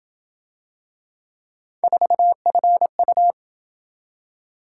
Morse code, heard as a keyed tone, about 28 wpm, 700 Hz. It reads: 4FU